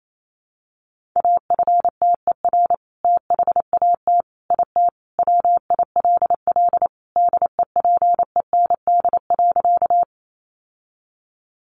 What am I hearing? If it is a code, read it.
Morse code, heard as a keyed tone, about 28 wpm, 700 Hz. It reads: AFTER THAT IT WILL DEPEND.
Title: Sentences from Top 1300 Words 28wpm